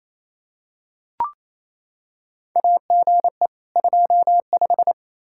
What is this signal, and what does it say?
Morse code, heard as a keyed tone, about 28 wpm, 700 Hz. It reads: AGE 25
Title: Practice QSO - Exchange 3 - More Info - 28wpm